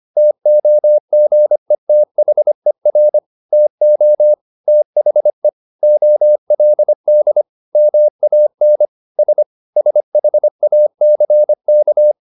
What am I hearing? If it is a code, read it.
Morse code, heard as a keyed tone, about 25 wpm, 600 Hz. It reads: TOGETHER TO THE OLD MAN S SHACK
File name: the old man and the sea part 1 25wpm